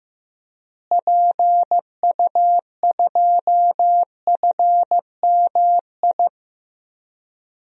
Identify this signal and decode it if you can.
Morse code, heard as a keyed tone, about 15 wpm, 700 Hz. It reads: PU2FMI
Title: DXCC-100 Call Signs - 6 Characters - Repeated 3x - 15wpm